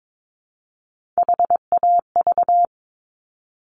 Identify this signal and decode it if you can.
Morse code, heard as a keyed tone, about 22 wpm, 700 Hz. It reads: HAV
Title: Top Three Letter Combinations in Words 22wpm